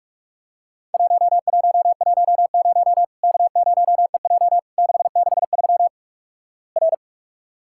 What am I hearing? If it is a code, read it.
Morse code, heard as a keyed tone, about 48 wpm, 700 Hz. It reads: 1110 K0EJ 663 R
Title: Worked All Europe DX Contest - QTC Exchange - 45wpm